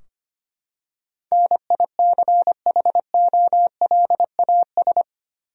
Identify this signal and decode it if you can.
Morse code, heard as a keyed tone, about 25 wpm, 700 Hz. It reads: NICHOLAS
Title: USA Common Names - 25wpm